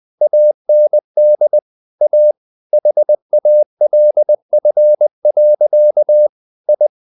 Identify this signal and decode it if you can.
Morse code, heard as a keyed tone, about 20 wpm, 600 Hz. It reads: AND A HALF. I
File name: the old man and the sea / the old man and the sea part 1 20wpm